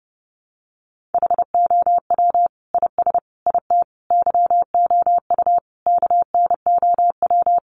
Morse code, heard as a keyed tone, about 30 wpm, 700 Hz. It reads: HOW IS IT YOU KNOW
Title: Sentences from Top 700 Words 30wpm